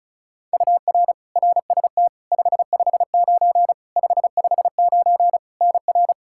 Morse code, heard as a keyed tone, about 36 wpm, 700 Hz. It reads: UR RST 559 559 NR
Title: Practice QSO - Exchange 1 - Beginning - 35wpm